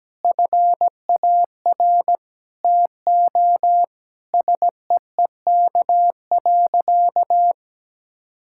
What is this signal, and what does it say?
Morse code, heard as a keyed tone, about 17 wpm, 700 Hz. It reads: FAR TO SEEK.
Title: Sentences from Top 1600 Words 17wpm